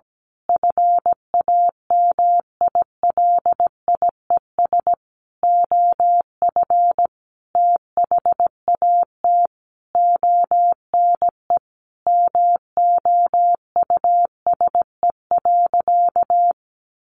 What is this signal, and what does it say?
Morse code, heard as a keyed tone, about 17 wpm, 700 Hz. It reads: FAMILIES OF THAT ONE MOUSE.
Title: A Sound Of Thunder 17wpm